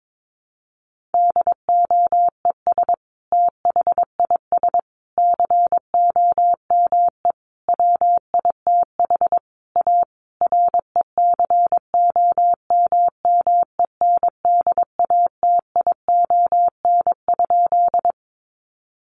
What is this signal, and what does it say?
Morse code, heard as a keyed tone, about 22 wpm, 700 Hz. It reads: DOES THIS COME WITH A RECOMMENDATION?
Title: Sentences from Top 1400 Words 22wpm